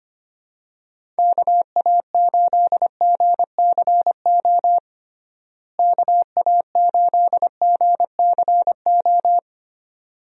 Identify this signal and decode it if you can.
Morse code, heard as a keyed tone, about 25 wpm, 700 Hz. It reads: KA8GCO KA8GCO
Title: USA Call Signs - 6 Characters - Repeated 2x - 25wpm